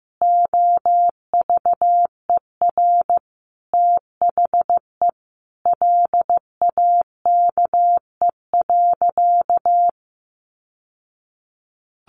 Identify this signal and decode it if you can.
Morse code, heard as a keyed tone, about 14 wpm, 700 Hz. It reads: OVER THE LAKE.
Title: Sentences from Top 1100 Words 15wpm